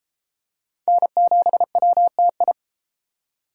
Morse code, heard as a keyed tone, about 31 wpm, 700 Hz. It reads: N7WTI